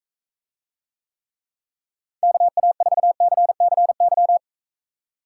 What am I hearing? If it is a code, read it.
Morse code, heard as a keyed tone, about 45 wpm, 700 Hz. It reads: KA4CCY